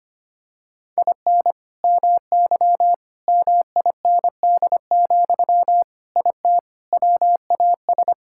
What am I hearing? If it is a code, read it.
Morse code, heard as a keyed tone, about 25 wpm, 700 Hz. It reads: IN MY MIND, IT WAS
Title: Sentences from Top 1000 Words - Review Part 6 - 25wpm